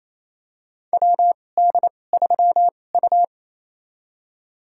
Morse code, heard as a keyed tone, about 28 wpm, spaced as for 20 wpm, 700 Hz. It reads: WD3U
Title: USA Call Signs - 4 Characters - 2x Character Spacing - 28wpm